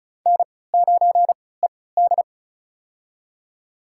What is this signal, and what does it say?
Morse code, heard as a keyed tone, about 36 wpm, spaced as for 20 wpm, 690 Hz. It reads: N9ED
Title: USA Call Signs - 4 Characters - 3x Character Spacing - 35wpm